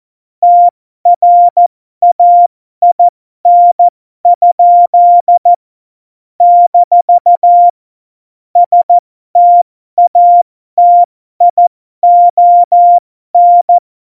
Morse code, heard as a keyed tone, about 14 wpm, spaced as for 12 wpm, 700 Hz. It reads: TRAIN? - STATION
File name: Morsekurs / Witze